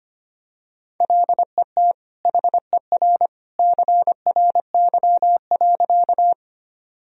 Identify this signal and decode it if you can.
Morse code, heard as a keyed tone, about 25 wpm, 700 Hz. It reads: LET HER CRY.
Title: Sentences from Top 1000 Words - Review Part 2 - 25wpm